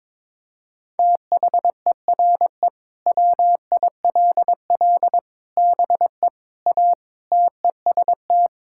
Morse code, heard as a keyed tone, about 22 wpm, 700 Hz. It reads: THERE WILL BE A TEST
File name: practice / 22wpm / Sentences with 3-Word Phrases - 22wpm